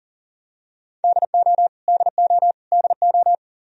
Morse code, heard as a keyed tone, about 40 wpm, 700 Hz. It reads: DO DO DO
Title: Top 100 Words Repeated 40wpm